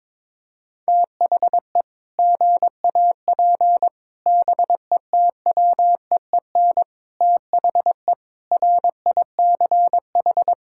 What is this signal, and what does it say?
Morse code, heard as a keyed tone, about 22 wpm, 700 Hz. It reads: THE GAP BETWEEN THE RICH
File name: practice / 22wpm / Sentences with 8-Word Phrases - 22wpm